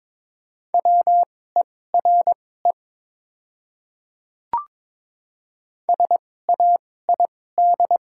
Morse code, heard as a keyed tone, about 21 wpm, spaced as for 16 wpm, 700 Hz. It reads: WERE SAID
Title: 2x Character Spacing - Top 100 Words - 22wpm